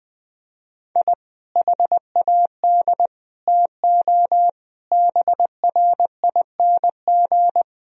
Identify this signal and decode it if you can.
Morse code, heard as a keyed tone, about 20 wpm, 700 Hz. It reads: I HAD TO BRING
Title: Sentences from Top 1000 Words - Review Part 3 - 20wpm